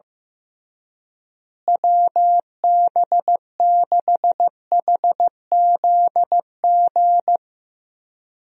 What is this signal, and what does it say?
Morse code, heard as a keyed tone, about 15 wpm, 700 Hz. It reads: WB6HZG
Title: USA Call Signs - 6 Characters - Repeated 3x - 15wpm